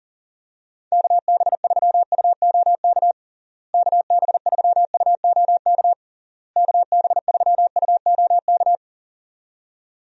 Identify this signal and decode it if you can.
Morse code, heard as a keyed tone, about 40 wpm, 700 Hz. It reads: KB3UOK KB3UOK KB3UOK
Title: USA Call Signs - 6 Characters - Repeated 3x - 40wpm